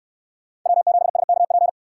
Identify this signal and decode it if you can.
Morse code, heard as a keyed tone, about 65 wpm, 700 Hz. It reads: W7SKP